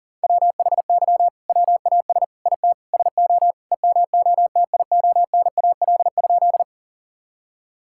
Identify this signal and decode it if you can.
Morse code, heard as a keyed tone, about 40 wpm, 700 Hz. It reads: WHY WAS IT SO EMOTIONAL?
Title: Sentences from Top 1000 Words - Review Part 5 - 40wpm